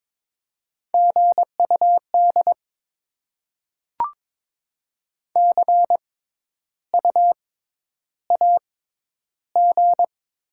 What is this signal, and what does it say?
Morse code, heard as a keyed tone, about 22 wpm, spaced as for 10 wpm, 700 Hz. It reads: GUD CUAG
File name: Top 20 QSO Elements - 6x Character Spacing - 22wpm